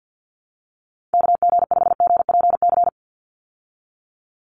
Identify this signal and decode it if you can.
Morse code, heard as a keyed tone, about 50 wpm, 700 Hz. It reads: KG5GPC
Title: USA Call Signs 50wpm